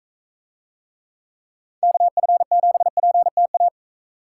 Morse code, heard as a keyed tone, about 42 wpm, 700 Hz. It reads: KF7PTA